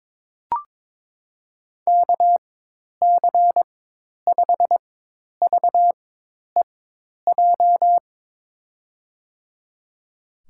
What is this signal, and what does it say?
Morse code, heard as a keyed tone, about 22 wpm, spaced as for 10 wpm, 700 Hz. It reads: KC5VEJ
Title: USA Call Signs - 6 Characters - 4x Character Spacing - 22wpm